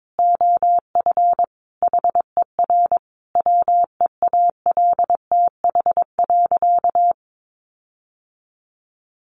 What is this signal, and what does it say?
Morse code, heard as a keyed tone, about 22 wpm, 700 Hz. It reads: OF HER WEALTH.
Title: Sentences from Top 1500 Words 22wpm